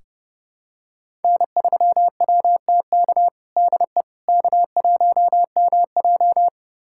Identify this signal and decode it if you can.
Morse code, heard as a keyed tone, about 30 wpm, 700 Hz. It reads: N3WTK DE K1MJ
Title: USA Callsign DE Callsign 30wpm